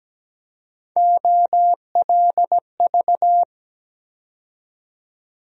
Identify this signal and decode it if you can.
Morse code, heard as a keyed tone, about 17 wpm, 700 Hz. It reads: OLV
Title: USA Call Sign Postfixes - 3 Letters 17wpm